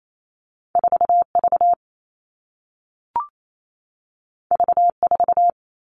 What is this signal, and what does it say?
Morse code, heard as a keyed tone, about 28 wpm, 700 Hz. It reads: 4V V4
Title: All Cut Numbers - 2-Digit 28wpm